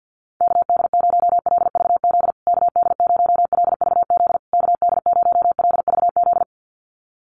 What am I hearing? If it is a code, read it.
Morse code, heard as a keyed tone, about 48 wpm, 700 Hz. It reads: KD0LVZ KD0LVZ KD0LVZ
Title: USA Call Signs Repeated 3x 50wpm